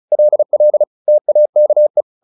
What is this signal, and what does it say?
Morse code, heard as a keyed tone, about 34 wpm, 600 Hz. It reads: LL TAKE
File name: the old man and the sea / the old man and the sea 1 35wpm